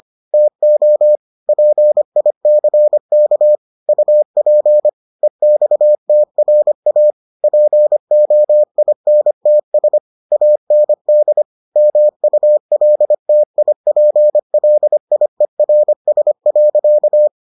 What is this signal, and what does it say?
Morse code, heard as a keyed tone, about 25 wpm, 600 Hz. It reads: TO PICK UP EXTRA POINTS AND MULTIPLIERS.